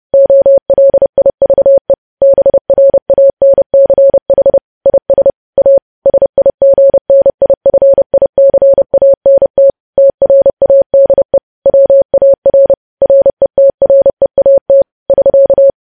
Morse code, heard as a keyed tone, about 30 wpm, 550 Hz. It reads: OLIVE BRANCH IS A SIGNIFICANT TRADE WAR RETREAT <SK>
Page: news headlines in morse code at 30 wpm